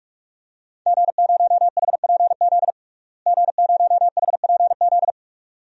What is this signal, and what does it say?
Morse code, heard as a keyed tone, about 41 wpm, 700 Hz. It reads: G0HPZ G0HPZ
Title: DXCC-100 Call Signs - 5 Characters - Repeated 2x - 45wpm